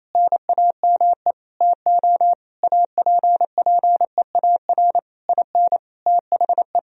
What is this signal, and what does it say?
Morse code, heard as a keyed tone, about 28 wpm, 700 Hz. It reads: NAME TO APPEAR IN THE
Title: Sentences from Top 1000 Words - Review Part 2 - 28wpm